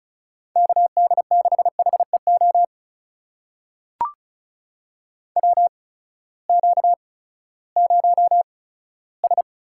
Morse code, heard as a keyed tone, about 36 wpm, spaced as for 18 wpm, 700 Hz. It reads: KD6HEO WQ0S
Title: USA Call Signs - 8x Character Spacing - 35wpm